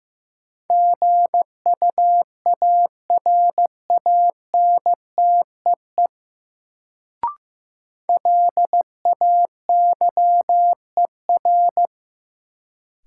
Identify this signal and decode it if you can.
Morse code, heard as a keyed tone, about 15 wpm, 700 Hz. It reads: GUARANTEE LAYER
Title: Sentences from Top 1600 Words - New Word Drill Alt - 15wpm